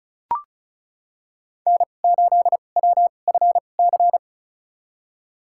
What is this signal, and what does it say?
Morse code, heard as a keyed tone, about 36 wpm, spaced as for 25 wpm, 700 Hz. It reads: N8WFC
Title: USA Call Signs - 5 Characters - 2x Character Spacing - 35wpm